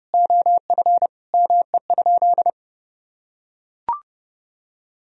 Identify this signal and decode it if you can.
Morse code, heard as a keyed tone, about 32 wpm, 700 Hz. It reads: OF ME?
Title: Sentences from Top 1000 Words - Review Part 3 - 30wpm